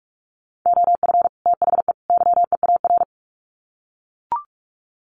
Morse code, heard as a keyed tone, about 44 wpm, 700 Hz. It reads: OF THE YEAR
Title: Sets of 3 Words 45wpm